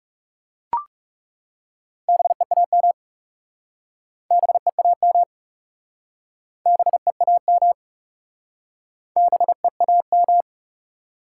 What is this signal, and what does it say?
Morse code, heard as a keyed tone, about 41 wpm, 700 Hz. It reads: BEAM BEAM BEAM BEAM